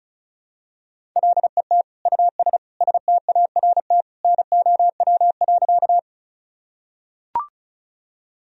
Morse code, heard as a keyed tone, about 36 wpm, 700 Hz. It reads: LET US START NOW.
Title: Sentences from Top 1000 Words - Review Part 1 - 35wpm